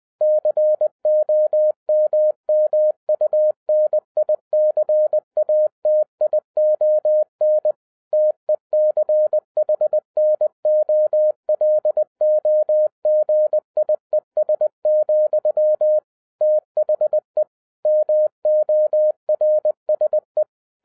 Morse code, heard as a keyed tone, about 20 wpm, 610 Hz. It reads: COMMUNICATION TECHNOLOGIES, THE MORSE